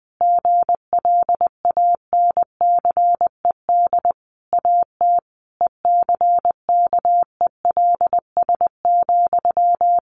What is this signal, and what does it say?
Morse code, heard as a keyed tone, about 20 wpm, 700 Hz. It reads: GLANCED AT ECKELS,
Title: A Sound Of Thunder 20wpm